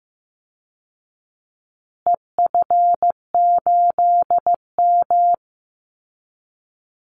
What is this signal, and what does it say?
Morse code, heard as a keyed tone, about 15 wpm, 700 Hz. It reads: EF8M
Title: Call Signs Repeated 2x - 15wpm